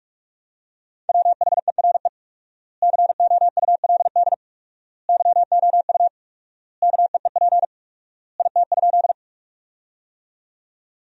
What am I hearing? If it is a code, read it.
Morse code, heard as a keyed tone, about 43 wpm, 700 Hz. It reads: WHERE COULD YOU KEEP IT?
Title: 4x Word-Spacing - Sentences from Top 300 Words - 45wpm